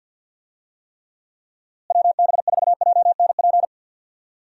Morse code, heard as a keyed tone, about 52 wpm, 700 Hz. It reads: WB4JNP
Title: USA Call Signs - 6 Characters - Repeated 3x - 50wpm